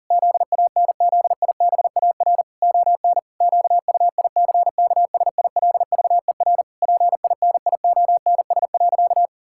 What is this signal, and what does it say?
Morse code, heard as a keyed tone, about 40 wpm, 700 Hz. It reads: ZANZIBAR ON QUICKSILVER PINIONS.
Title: Pangram Challenge v2 - 40wpm